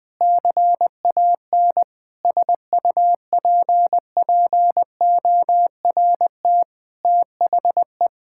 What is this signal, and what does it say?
Morse code, heard as a keyed tone, about 20 wpm, 700 Hz. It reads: CAN SUPPORT THE
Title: Sentences with 9-Word Phrases - 20wpm